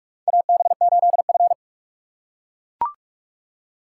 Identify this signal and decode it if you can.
Morse code, heard as a keyed tone, about 43 wpm, 700 Hz. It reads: AB8F
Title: USA Call Signs - 4 Characters - 2x Character Spacing - 45wpm